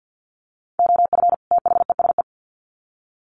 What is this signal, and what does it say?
Morse code, heard as a keyed tone, about 50 wpm, 700 Hz. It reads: OF THESE
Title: Sets of 2 Words 50wpm